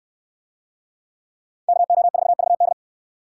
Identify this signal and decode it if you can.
Morse code, heard as a keyed tone, about 68 wpm, 700 Hz. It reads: KO4VD